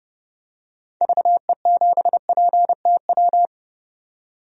Morse code, heard as a keyed tone, about 31 wpm, 700 Hz. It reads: VE7PTW
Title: DXCC-100 Call Signs - 6 Characters - 30wpm